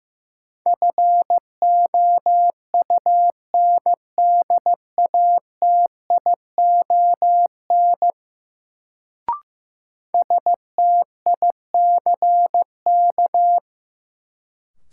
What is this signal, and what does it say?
Morse code, heard as a keyed tone, about 15 wpm, 700 Hz. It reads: FOUNDATION STICK
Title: Sentences from Top 1200 Words - New Word Drill - 15wpm